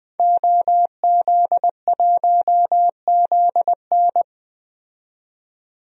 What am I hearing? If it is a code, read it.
Morse code, heard as a keyed tone, about 21 wpm, 700 Hz. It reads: OZ1ZN
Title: DXCC-100 Call Signs - 5 Characters - 20wpm